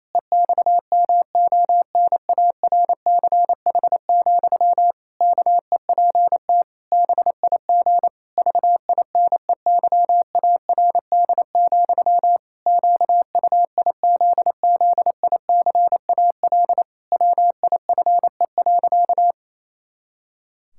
Morse code, heard as a keyed tone, about 28 wpm, 700 Hz. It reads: EXMONARCH, KEPT BIG VINEYARD, QUIZZICAL WIFE.
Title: Pangram Challenge v2 - 28wpm